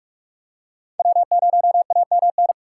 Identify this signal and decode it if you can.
Morse code, heard as a keyed tone, about 44 wpm, 700 Hz. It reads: W0AMN
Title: USA Call Signs - 5 Characters - 4x Character Spacing - 45wpm